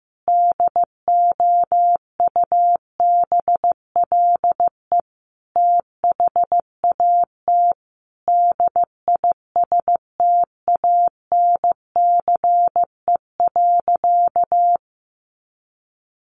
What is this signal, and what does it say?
Morse code, heard as a keyed tone, about 15 wpm, 700 Hz. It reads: DOUBLE THAT DISTANCE.
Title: Sentences from Top 1400 Words 15wpm